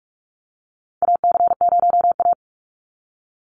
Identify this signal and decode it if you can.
Morse code, heard as a keyed tone, about 44 wpm, 700 Hz. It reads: AC0A
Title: NA SOTA Chaser Callsigns 45wpm